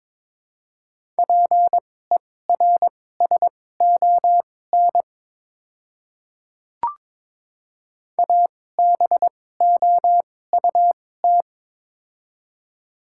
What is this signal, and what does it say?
Morse code, heard as a keyed tone, about 21 wpm, spaced as for 16 wpm, 700 Hz. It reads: PERSON ABOUT
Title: 2x Character Spacing - Top 500 Words - 22wpm